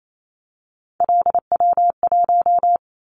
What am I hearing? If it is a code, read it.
Morse code, heard as a keyed tone, about 28 wpm, 700 Hz. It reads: LW1
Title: Callsign Prefixes for Top 50 DXCC - 28wpm